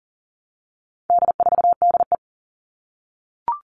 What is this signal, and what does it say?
Morse code, heard as a keyed tone, about 40 wpm, 700 Hz. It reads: D4DE
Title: All Cut Numbers - 4-Digit 40wpm